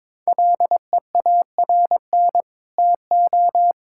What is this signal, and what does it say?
Morse code, heard as a keyed tone, about 22 wpm, 700 Hz. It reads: LEARN TO
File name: Sets of 9 Words - 22wpm